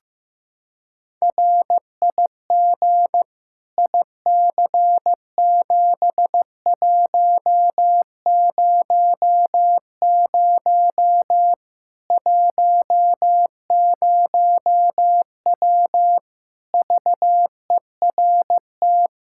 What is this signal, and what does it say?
Morse code, heard as a keyed tone, about 15 wpm, 700 Hz. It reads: RIG IC7100 10W VERT
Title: Practice QSO - Exchange 2 - Info - 15wpm